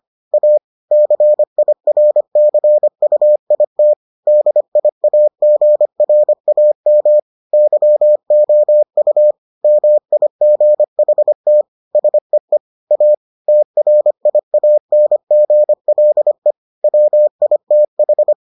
Morse code, heard as a keyed tone, about 25 wpm, 600 Hz. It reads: A CIRCUIT DIAGRAM YOU MIGHT SEE A TRIANGLE WITH